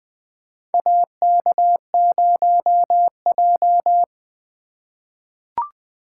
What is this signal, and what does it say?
Morse code, heard as a keyed tone, about 20 wpm, 700 Hz. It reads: AK0J
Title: USA Call Signs - 4 Characters - 8x Character Spacing - 20wpm